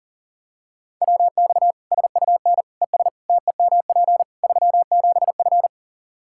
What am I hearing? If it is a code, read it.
Morse code, heard as a keyed tone, about 40 wpm, 700 Hz. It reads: WX SUN ES TEMP 37F
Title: K1USN SST QSO Practice 40wpm